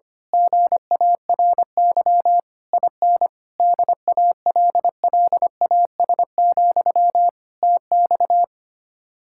Morse code, heard as a keyed tone, about 24 wpm, 700 Hz. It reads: GARY IN DALLAS, TX
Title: Speed-Racing Common QSO Phrases Encore - 25wpm